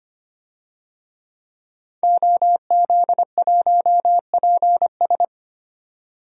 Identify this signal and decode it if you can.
Morse code, heard as a keyed tone, about 25 wpm, 700 Hz. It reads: OZ1PS
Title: DXCC-100 Call Signs - 5 Characters - 25wpm